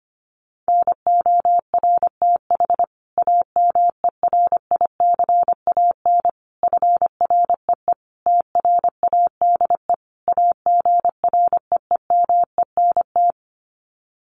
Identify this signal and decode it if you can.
Morse code, heard as a keyed tone, about 25 wpm, 700 Hz. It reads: NORTH AMERICAN FREE TRADE AGREEMENT
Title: Sets of 5 Words Encore 25wpm